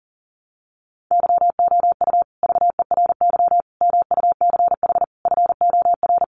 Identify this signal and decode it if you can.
Morse code, heard as a keyed tone, about 40 wpm, 700 Hz. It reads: YOU VERY MUCH FOR
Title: Sets of 5 Words Encore 40wpm